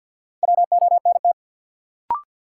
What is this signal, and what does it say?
Morse code, heard as a keyed tone, about 52 wpm, 700 Hz. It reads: WONT
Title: Sentences from Top 1000 Words - Review Part 6 - New Word Drill - 50wpm